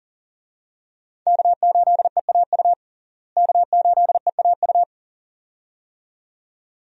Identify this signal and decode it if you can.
Morse code, heard as a keyed tone, about 40 wpm, 700 Hz. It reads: K8EAU K8EAU
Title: USA Call Signs - 5 Characters - Repeated 2x - 40wpm